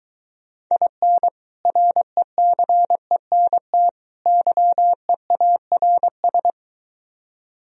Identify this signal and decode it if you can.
Morse code, heard as a keyed tone, about 23 wpm, 700 Hz. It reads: IN RECENT YEARS